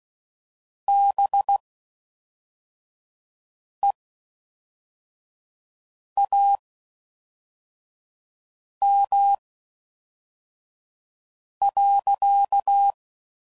Morse code, heard as a keyed tone, about 16 wpm, spaced as for 4 wpm, 800 Hz. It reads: BEAM.